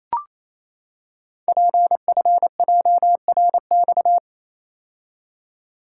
Morse code, heard as a keyed tone, about 28 wpm, 700 Hz. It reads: PFJRX
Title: Code Groups - 5 Letters - 28wpm